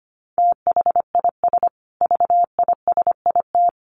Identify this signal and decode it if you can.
Morse code, heard as a keyed tone, about 25 wpm, 700 Hz. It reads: THIS VISIT
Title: Sentences from Top 1400 Words 25wpm